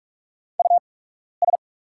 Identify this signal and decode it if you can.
Morse code, heard as a keyed tone, about 44 wpm, spaced as for 12 wpm, 690 Hz. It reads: US
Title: 8x Character Spacing - Top 300 Words - 45wpm